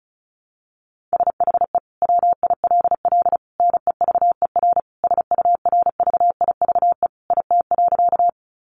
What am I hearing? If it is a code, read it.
Morse code, heard as a keyed tone, about 34 wpm, 700 Hz. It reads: SHE WILL NEVER SURVIVE IT.
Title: Sentences from Top 1300 Words 35wpm